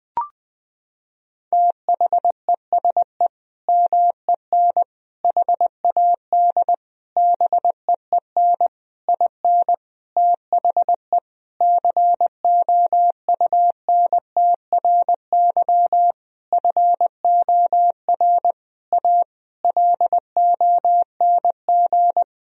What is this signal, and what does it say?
Morse code, heard as a keyed tone, about 20 wpm, 700 Hz. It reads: THESE MEN HAD BEEN IN THE COUNTRY FOR A LONG
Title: Sentences from Top 1000 Words - Review Part 1 - 20wpm